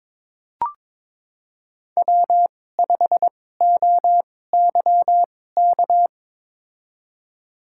Morse code, heard as a keyed tone, about 22 wpm, spaced as for 16 wpm, 700 Hz. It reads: W5OYK
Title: USA Call Signs - 5 Characters - 2x Character Spacing - 22wpm